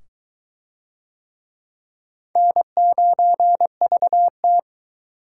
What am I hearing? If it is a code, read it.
Morse code, heard as a keyed tone, about 22 wpm, 700 Hz. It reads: N9VT